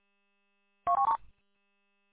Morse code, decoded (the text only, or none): none